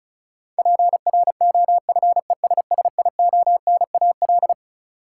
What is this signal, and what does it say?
Morse code, heard as a keyed tone, about 36 wpm, 700 Hz. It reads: PROFESSIONAL
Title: Sentences from Top 1000 Words - Review Part 4 - New Word Drill - 35wpm